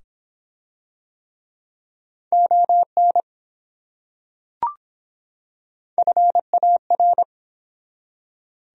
Morse code, heard as a keyed tone, about 24 wpm, 700 Hz. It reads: ON FAR